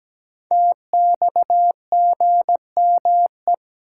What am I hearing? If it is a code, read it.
Morse code, heard as a keyed tone, about 17 wpm, 700 Hz. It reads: TXGME
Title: Code Groups - 5 Letter:Numbers - 17wpm